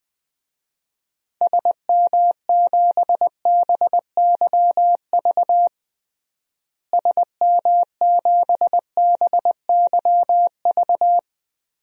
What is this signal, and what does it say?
Morse code, heard as a keyed tone, about 20 wpm, 700 Hz. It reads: SM7BYV SM7BYV
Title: DXCC-100 Call Signs - 6 Characters - Repeated 3x - 20wpm